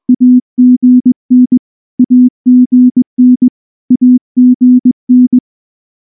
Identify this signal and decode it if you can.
Morse code, heard as a keyed tone, about 19 wpm, 250 Hz. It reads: AGN AGN AGN
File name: morse-training-20WPM